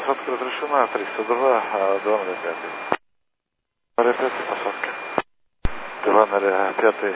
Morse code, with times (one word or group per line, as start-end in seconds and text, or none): none